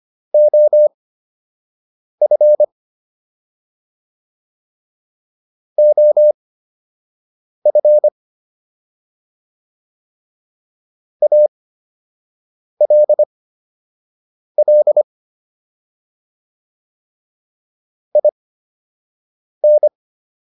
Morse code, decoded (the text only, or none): OF OF ALL IN